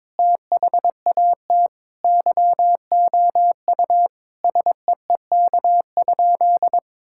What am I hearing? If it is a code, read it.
Morse code, heard as a keyed tone, about 22 wpm, 700 Hz. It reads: THAT YOU SEEK?
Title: Sentences from Top 1600 Words 22wpm